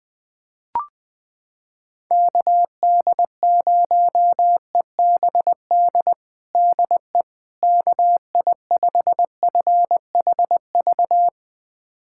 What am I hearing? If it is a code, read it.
Morse code, heard as a keyed tone, about 20 wpm, 700 Hz. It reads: KD0EBD DE KI5FHV
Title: USA Callsign DE Callsign - 6 Characters - 20wpm